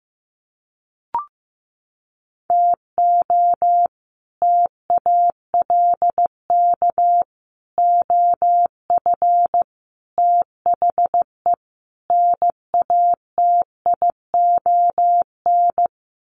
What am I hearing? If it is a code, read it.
Morse code, heard as a keyed tone, about 15 wpm, 700 Hz. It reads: TO TALK OF THE NATION
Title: Sets of 5 Words 15wpm